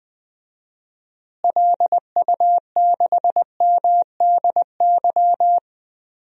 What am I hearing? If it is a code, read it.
Morse code, heard as a keyed tone, about 20 wpm, 700 Hz. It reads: LU6MDY
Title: DXCC-100 Call Signs - 6 Characters - Repeated 3x - 20wpm